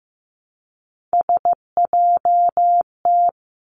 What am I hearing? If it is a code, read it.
Morse code, heard as a keyed tone, about 15 wpm, 700 Hz. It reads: SJT
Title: Three Challenging Letter-Number Combinations 15wpm